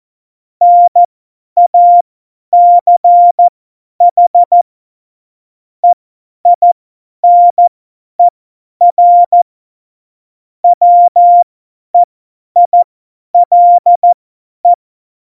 Morse code, heard as a keyed tone, about 14 wpm, spaced as for 10 wpm, 700 Hz. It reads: NACH EINER WEILE
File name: Morsekurs / Witze / Witz 15. Dezember 2020, 50 F70